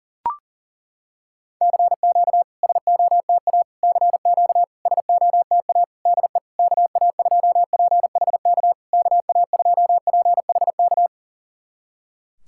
Morse code, heard as a keyed tone, about 40 wpm, 690 Hz. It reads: CQ SOTA CQ SOTA DE KA2PHK KA2PHK
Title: SOTA QSO Practice 40wpm